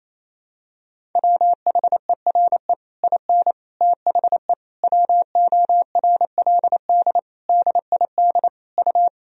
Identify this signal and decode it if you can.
Morse code, heard as a keyed tone, about 28 wpm, 700 Hz. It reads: WHERE IN THE WORLD DID U
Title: Sentences from Top 200 Words and Abbreviations 28wpm